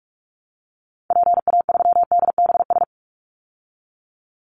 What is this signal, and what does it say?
Morse code, heard as a keyed tone, about 44 wpm, 700 Hz. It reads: PA3DBS
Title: Call Signs Repeated 3x - 45wpm